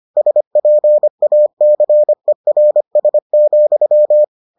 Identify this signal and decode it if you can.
Morse code, heard as a keyed tone, about 25 wpm, 600 Hz. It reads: SPACERS,